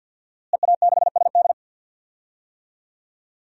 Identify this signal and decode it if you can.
Morse code, heard as a keyed tone, about 52 wpm, 700 Hz. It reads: EA6SD
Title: DXCC-100 Call Signs - 5 Characters - Repeated 3x - 50wpm